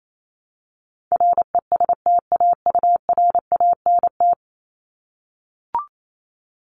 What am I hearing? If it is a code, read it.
Morse code, heard as a keyed tone, about 28 wpm, 700 Hz. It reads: RESTAURANT
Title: Top 1751-2000 Words 28wpm